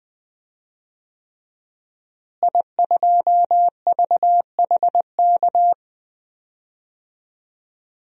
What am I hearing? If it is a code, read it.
Morse code, heard as a keyed tone, about 20 wpm, 700 Hz. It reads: I2VHK
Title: DXCC-100 Call Signs - 5 Characters - Repeated 3x - 20wpm